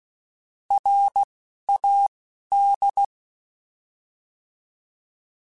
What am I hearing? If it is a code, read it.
Morse code, heard as a keyed tone, about 16 wpm, 800 Hz. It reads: R A D